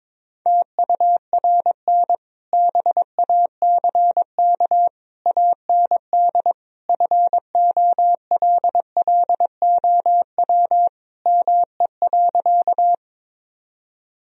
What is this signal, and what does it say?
Morse code, heard as a keyed tone, about 22 wpm, 700 Hz. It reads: TURN BACK AND FOLLOW ME.
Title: Sentences from Top 1000 Words - Review Part 1 - 22wpm